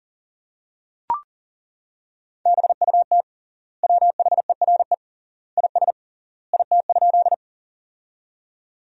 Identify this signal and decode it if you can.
Morse code, heard as a keyed tone, about 37 wpm, 700 Hz. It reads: BUT WHERE IS IT?
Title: 3x Word-Spacing - Sentences from Top 200 Words - 40wpm